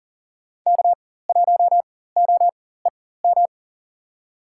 Morse code, heard as a keyed tone, about 40 wpm, spaced as for 19 wpm, 700 Hz. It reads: K1OEM
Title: USA Call Signs - 5 Characters - 4x Character Spacing - 40wpm